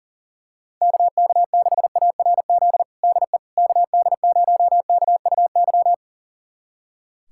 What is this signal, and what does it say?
Morse code, heard as a keyed tone, about 40 wpm, 700 Hz. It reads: KK6ARZ DE KD0KUY
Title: USA Callsign DE Callsign 40wpm